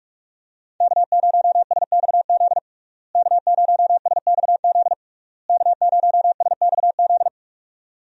Morse code, heard as a keyed tone, about 44 wpm, 700 Hz. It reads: K0SXZ K0SXZ K0SXZ
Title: USA Call Signs - 5 Characters - Repeated 3x - 45wpm